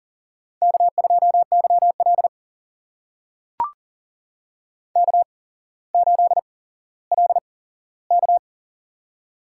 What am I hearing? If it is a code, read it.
Morse code, heard as a keyed tone, about 40 wpm, spaced as for 17 wpm, 700 Hz. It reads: K2YL K8LK
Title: USA Call Signs - 4 Characters - 8x Character Spacing - 40wpm